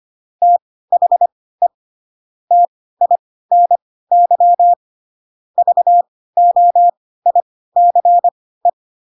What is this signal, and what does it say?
Morse code, heard as a keyed tone, about 25 wpm, spaced as for 16 wpm, 700 Hz. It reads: THE TINY VOICE
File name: Pinocchio / Pinocchio ch1 10 25 ik1hge